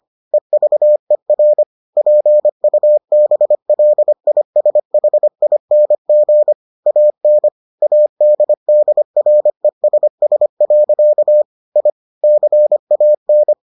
Morse code, heard as a keyed tone, about 25 wpm, 600 Hz. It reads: EVER PUBLISHING AN ADDRESS. I CAN